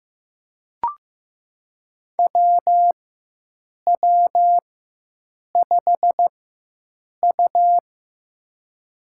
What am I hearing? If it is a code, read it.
Morse code, heard as a keyed tone, about 15 wpm, spaced as for 7 wpm, 700 Hz. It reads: WW5U
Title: USA Call Signs - 4 Characters - 4x Character Spacing - 15wpm